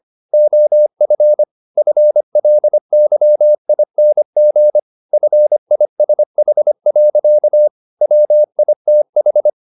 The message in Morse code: OF FLYING FISH. WITH

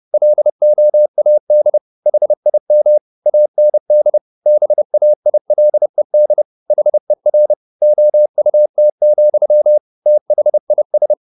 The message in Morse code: LOAD HIM AND BAILED HER OUT, THIS

